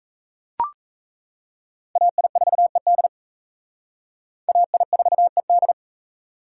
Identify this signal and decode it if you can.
Morse code, heard as a keyed tone, about 35 wpm, 700 Hz. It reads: AI4ED AI4ED
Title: Speed Racing - USA 2x2 Call Signs 28wpm